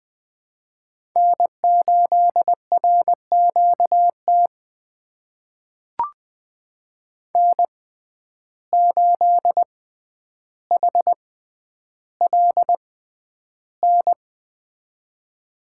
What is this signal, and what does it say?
Morse code, heard as a keyed tone, about 20 wpm, spaced as for 10 wpm, 700 Hz. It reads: N8RQT N8HLN
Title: USA Call Signs - 5 Characters - 6x Character Spacing - 20wpm